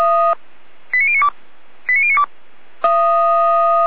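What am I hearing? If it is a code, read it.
Five-tone selective call (ZVEI), heard as quick runs of five tones, tones of about 70 ms. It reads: 89092 89092
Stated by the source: ZVEI five-tone call received on SDR.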